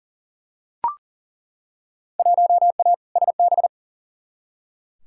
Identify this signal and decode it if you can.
Morse code, heard as a keyed tone, about 40 wpm, 700 Hz. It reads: 1A SB
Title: Field Day Exchanges Encore - 40wpm